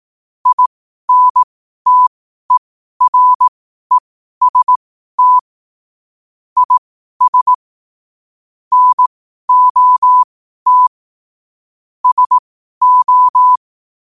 Morse code, heard as a keyed tone, about 18 wpm, spaced as for 12 wpm, 1000 Hz. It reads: INTEREST IS NOT SO
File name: qotdfiles / 12wpm